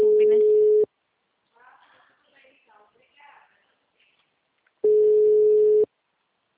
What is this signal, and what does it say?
Ringback tone (Europe, 425 Hz), with speech.